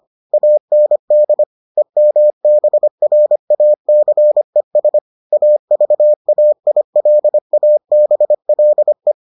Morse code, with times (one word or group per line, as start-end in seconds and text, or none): 0.3-1.4 AND
1.8-5.0 EMBRACES
5.3-9.1 AVAILABLE